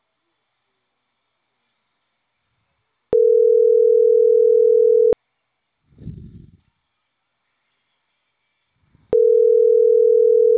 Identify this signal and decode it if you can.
Ringback tone (North America), 440 + 480 Hz.